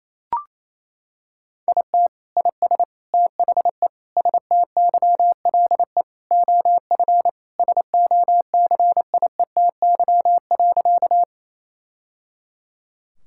Morse code, heard as a keyed tone, about 28 wpm, 700 Hz. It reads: IT IS THE STYLE OF SOCIETY.
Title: Sentences from Top 1000 Words - Review Part 5 - 28wpm